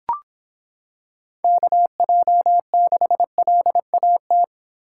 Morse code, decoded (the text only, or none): KJ6LAT